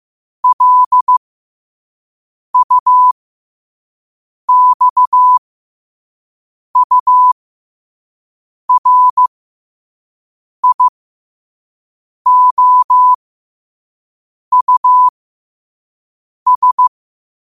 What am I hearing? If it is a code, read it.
Morse code, heard as a keyed tone, about 15 wpm, spaced as for 5 wpm, 1000 Hz. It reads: LUXURIOUS
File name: qotdfiles / 5wpm